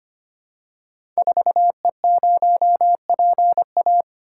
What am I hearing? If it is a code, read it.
Morse code, heard as a keyed tone, about 25 wpm, 700 Hz. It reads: 4E0PA
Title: Code Groups - 5 Letter:Numbers - 25wpm